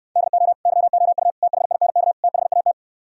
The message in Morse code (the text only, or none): DO YOU THINK THAT